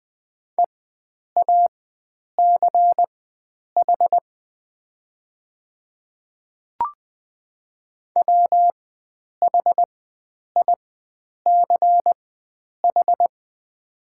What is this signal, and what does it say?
Morse code, heard as a keyed tone, about 20 wpm, spaced as for 9 wpm, 700 Hz. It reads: EACH WHICH
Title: 4x Character Spacing - Top 100 Words - 20wpm